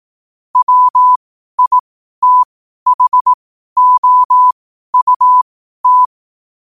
Morse code, heard as a keyed tone, about 18 wpm, spaced as for 12 wpm, 1000 Hz. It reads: WITHOUT
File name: qotdfiles / 12wpm